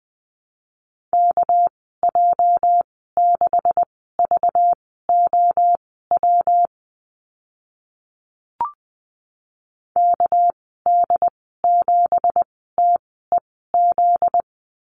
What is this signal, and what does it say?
Morse code, heard as a keyed tone, about 20 wpm, spaced as for 14 wpm, 700 Hz. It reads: KJ6VOW KD7TEZ
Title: 245 Lesson 39g - Letter z - Call Signs 20wpm